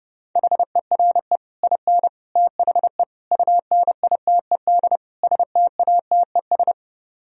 Morse code, heard as a keyed tone, about 30 wpm, 700 Hz. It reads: HERE IN THE UNITED STATES